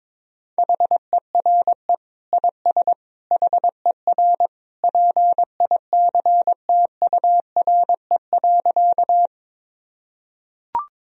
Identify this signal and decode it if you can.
Morse code, heard as a keyed tone, about 22 wpm, 700 Hz. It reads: HERE IS HER PICTURE.
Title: Sentences from Top 1000 Words - Review Part 1 - 22wpm